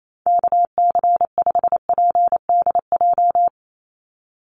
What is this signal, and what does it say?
Morse code, heard as a keyed tone, about 28 wpm, 700 Hz. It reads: KC5PDJ
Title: USA Call Signs 28wpm